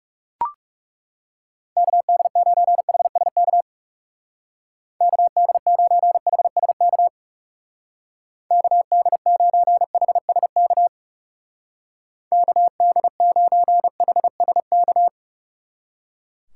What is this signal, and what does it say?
Morse code, heard as a keyed tone, about 39 wpm, 700 Hz. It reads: KD9HSK KD9HSK KD9HSK KD9HSK